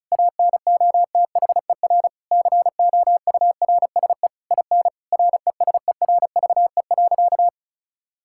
Morse code, heard as a keyed tone, about 36 wpm, 700 Hz. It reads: ANOTHER COURSE IN RESERVE.
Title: Sentences from Top 1000 Words - Review Part 6 - 35wpm